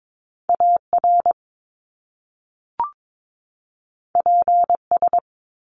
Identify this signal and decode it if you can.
Morse code, heard as a keyed tone, about 22 wpm, 700 Hz. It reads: AR PS
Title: Top Two Letter Combinations in Words 22wpm